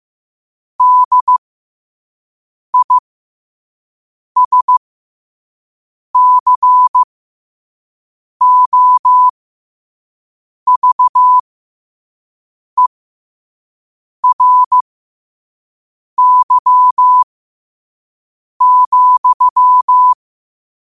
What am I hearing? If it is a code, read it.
Morse code, heard as a keyed tone, about 15 wpm, spaced as for 5 wpm, 1000 Hz. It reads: DISCOVERY,
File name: qotdfiles / 5wpm